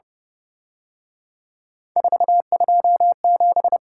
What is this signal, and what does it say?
Morse code, heard as a keyed tone, about 30 wpm, 700 Hz. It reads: 427
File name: Speed-Racing - Three Letter-Number Combinations - 25wpm